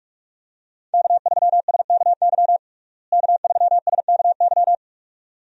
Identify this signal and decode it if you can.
Morse code, heard as a keyed tone, about 44 wpm, 700 Hz. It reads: K3SKY K3SKY
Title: USA Call Signs - 5 Characters - Repeated 2x - 45wpm